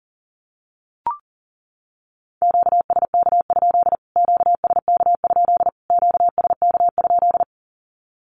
Morse code, heard as a keyed tone, about 40 wpm, 700 Hz. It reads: QSK? QSK? QSK?
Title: Q-Codes Repeated 3x 40wpm